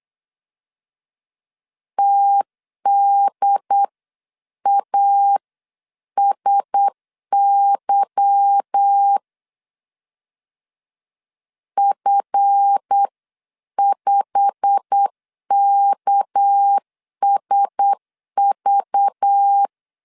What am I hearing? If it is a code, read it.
Morse code, heard as a keyed tone, about 8 wpm, 800 Hz. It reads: TDASY F5KSV
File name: CW / DET / 1 9 1500C S B 8wpm 5624